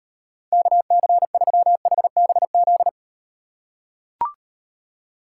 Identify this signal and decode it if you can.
Morse code, heard as a keyed tone, about 38 wpm, 700 Hz. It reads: KC3HBZ